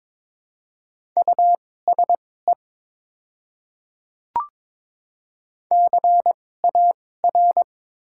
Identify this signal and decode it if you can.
Morse code, heard as a keyed tone, about 22 wpm, spaced as for 16 wpm, 700 Hz. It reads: USE CAR
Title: 2x Character Spacing - Top 300 Words - 22wpm